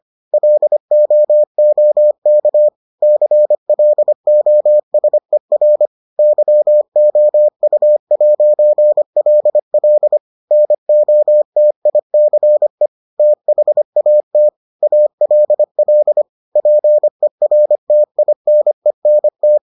Morse code, heard as a keyed tone, about 25 wpm, 600 Hz. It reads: LOOK CLOSER YOU'LL NOTICE THAT ALL PERTINENT